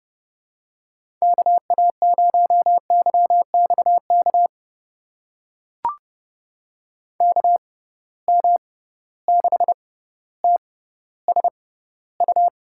Morse code, heard as a keyed tone, about 32 wpm, spaced as for 16 wpm, 700 Hz. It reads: KA0YXK KM6TSU
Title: USA Call Signs - 6 Characters - 6x Character Spacing - 30wpm